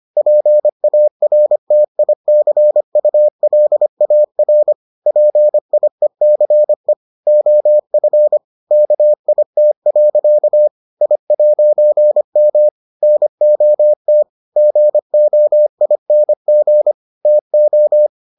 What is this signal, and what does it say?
Morse code, heard as a keyed tone, about 25 wpm, 600 Hz. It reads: PARTICULAR PIECE OF KIT. I'M NOT GOING TO